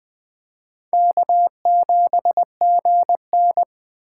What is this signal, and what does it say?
Morse code, heard as a keyed tone, about 21 wpm, 700 Hz. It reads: K7GN